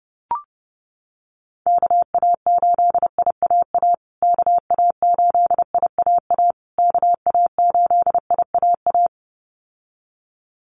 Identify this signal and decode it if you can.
Morse code, heard as a keyed tone, about 30 wpm, 700 Hz. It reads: KA8IAA KA8IAA KA8IAA
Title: USA Call Signs Repeated 3x 30wpm